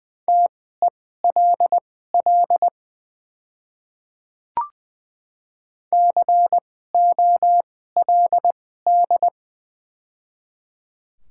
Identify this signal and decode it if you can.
Morse code, heard as a keyed tone, about 20 wpm, spaced as for 14 wpm, 700 Hz. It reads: TELL COLD
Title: 2x Character Spacing - Top 500 Words - 20wpm